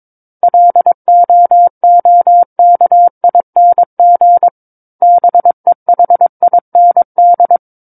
Morse code, heard as a keyed tone, about 22 wpm, 700 Hz. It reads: LOOKING BEHIND
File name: Book One / 13-Chapter 13 - 22 WPM